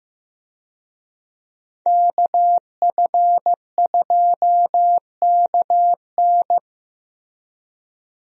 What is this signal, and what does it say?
Morse code, heard as a keyed tone, about 16 wpm, 700 Hz. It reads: KF2KN
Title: USA Call Signs - 5 Characters - Repeated 2x - 15wpm